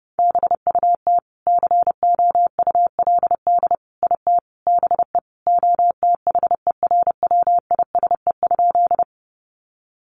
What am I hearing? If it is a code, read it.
Morse code, heard as a keyed tone, about 28 wpm, 700 Hz. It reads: BUT COULD IT BE OTHERWISE?
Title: Sentences from Top 1200 Words 30wpm